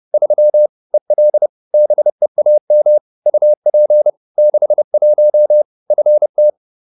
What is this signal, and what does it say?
Morse code, heard as a keyed tone, about 30 wpm, 600 Hz. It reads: 3 EL BEAM UP 61 FT
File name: QSO310 30